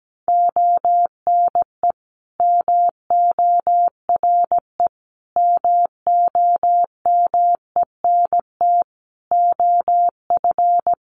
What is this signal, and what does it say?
Morse code, heard as a keyed tone, about 17 wpm, 700 Hz. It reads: ONE MORE MOMENT OF
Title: Sentences from Top 1300 Words 17wpm